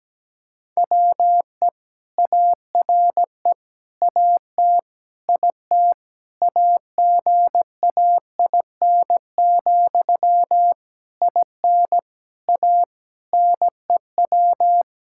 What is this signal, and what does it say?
Morse code, heard as a keyed tone, about 17 wpm, 700 Hz. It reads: WE ARE AT IT AGAIN, IN A NEW
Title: Sentences from Top 1000 Words - Review Part 5 - 17wpm